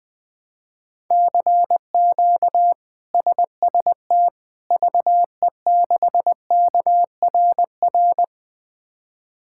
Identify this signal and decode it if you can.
Morse code, heard as a keyed tone, about 20 wpm, 700 Hz. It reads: CQ SST VE6KRR
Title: K1USN SST QSO Practice - 20wpm